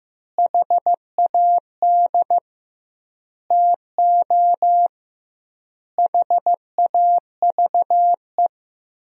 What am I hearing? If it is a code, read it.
Morse code, heard as a keyed tone, about 16 wpm, 700 Hz. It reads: HAD TO HAVE
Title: 2x Word-Spacing - Sentences from Top 100 Words - 15wpm